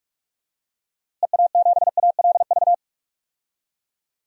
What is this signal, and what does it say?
Morse code, heard as a keyed tone, about 44 wpm, 700 Hz. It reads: EA7ALV
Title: DXCC-100 Call Signs - 6 Characters - Repeated 3x - 45wpm